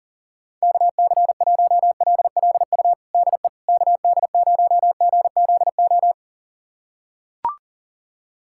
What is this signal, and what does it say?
Morse code, heard as a keyed tone, about 40 wpm, 700 Hz. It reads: KC1LLU DE KD0GZO
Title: USA Callsign DE Callsign 40wpm